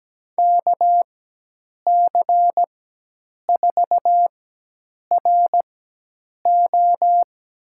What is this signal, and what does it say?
Morse code, heard as a keyed tone, about 17 wpm, spaced as for 8 wpm, 700 Hz. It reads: KC4RO
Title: USA Call Signs - 4x Character Spacing - 17wpm